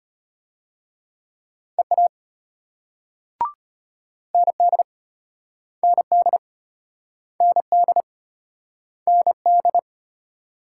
Morse code, heard as a keyed tone, about 35 wpm, 700 Hz. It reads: EA ND ND ND ND